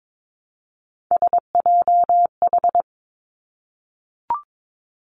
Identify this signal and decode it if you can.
Morse code, heard as a keyed tone, about 22 wpm, 700 Hz. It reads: SJH
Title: Three Challenging Letter-Number Combinations 22wpm